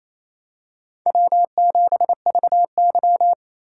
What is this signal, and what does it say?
Morse code, heard as a keyed tone, about 28 wpm, 700 Hz. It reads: W7VY
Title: USA Call Signs - 4 Characters - 3x Character Spacing - 28wpm